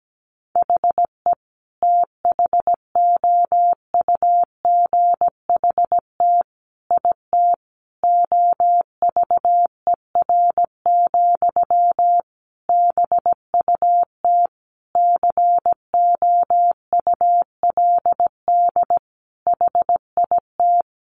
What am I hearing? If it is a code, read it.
Morse code, heard as a keyed tone, about 17 wpm, 700 Hz. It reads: HE THOUGHT IT OVER, BUT COULD HIT
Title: Sentences from Top 1200 Words 17wpm